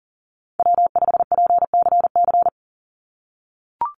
Morse code, heard as a keyed tone, about 40 wpm, 700 Hz. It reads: W5PCC
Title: USA Call Signs 40wpm